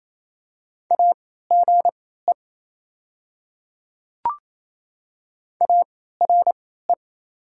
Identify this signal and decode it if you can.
Morse code, heard as a keyed tone, about 28 wpm, spaced as for 16 wpm, 700 Hz. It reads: AGE ARE
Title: 3x Character Spacing - Top 500 Words - 28wpm